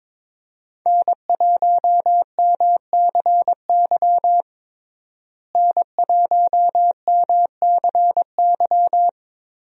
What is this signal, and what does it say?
Morse code, heard as a keyed tone, about 22 wpm, 700 Hz. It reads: N1MCY N1MCY
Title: USA Call Signs - 5 Characters - Repeated 3x - 22wpm